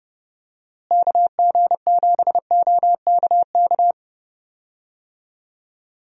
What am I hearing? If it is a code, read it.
Morse code, heard as a keyed tone, about 30 wpm, 700 Hz. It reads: KG7OKK
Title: USA Call Signs - 6 Characters - Repeated 2x - 30wpm